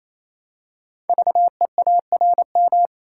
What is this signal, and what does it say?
Morse code, heard as a keyed tone, about 27 wpm, 700 Hz. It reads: VEARM
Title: Code Groups - 5 Letters - 28wpm